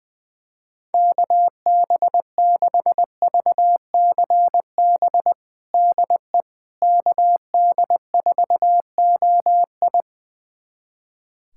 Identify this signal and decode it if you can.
Morse code, heard as a keyed tone, about 21 wpm, 700 Hz. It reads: KB6VCB DE KD4OI
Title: USA Callsign DE Callsign 20wpm